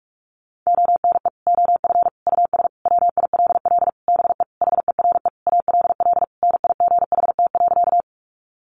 Morse code, heard as a keyed tone, about 45 wpm, 700 Hz. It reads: ONE OF US WILL BE HERE ALL NIGHT.
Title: Sentences from Top 300 Words 45wpm